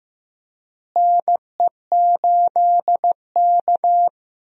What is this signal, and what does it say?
Morse code, heard as a keyed tone, about 15 wpm, 700 Hz. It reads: NE8K
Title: USA Call Signs - 4 Characters - Repeated 3x - 15wpm